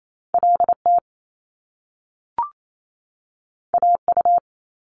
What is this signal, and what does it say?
Morse code, heard as a keyed tone, about 28 wpm, 700 Hz. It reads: LT AU
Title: Top Two Letter Combinations in Words 28wpm